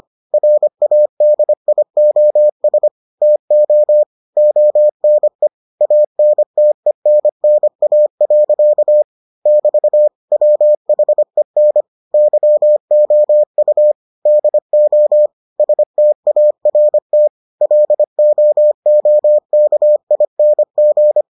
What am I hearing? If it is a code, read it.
Morse code, heard as a keyed tone, about 25 wpm, 600 Hz. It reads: RADIOS TO ONE ANTENNA. <BT> WHEN YOU DO START LOOKING